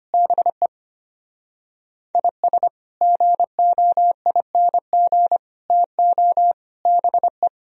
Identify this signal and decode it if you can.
Morse code, heard as a keyed tone, about 33 wpm, 700 Hz. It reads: BE IS GOING TO BE